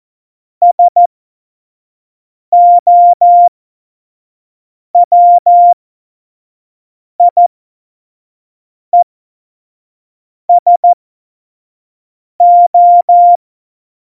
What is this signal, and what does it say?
Morse code, heard as a keyed tone, about 14 wpm, spaced as for 5 wpm, 700 Hz. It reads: SOWIESO
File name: Morsekurs / Witze / Witz 26. November 2019, 25 F70